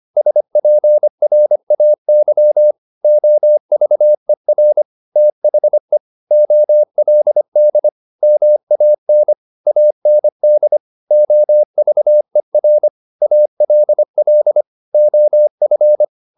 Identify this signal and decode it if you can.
Morse code, heard as a keyed tone, about 25 wpm, 600 Hz. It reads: SPRAY OVER THE OLD MAN AND OVER ALL OF